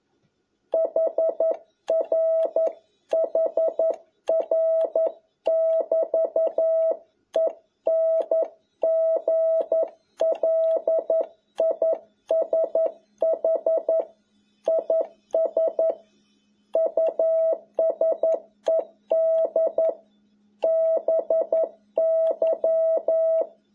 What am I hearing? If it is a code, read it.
Morse code, heard as a keyed tone, about 11 wpm, 650 Hz. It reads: HR HR <BT> ENGLISH IS USED BY
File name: jh8cbh / 2021cw